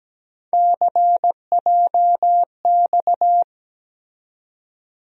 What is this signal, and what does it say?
Morse code, heard as a keyed tone, about 17 wpm, 700 Hz. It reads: CJX
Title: USA Call Sign Postfixes - 3 Letters 17wpm